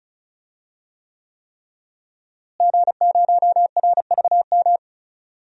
Morse code, heard as a keyed tone, about 34 wpm, 700 Hz. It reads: G0RVM
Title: DXCC-100 Call Signs - 5 Characters - Repeated 3x - 35wpm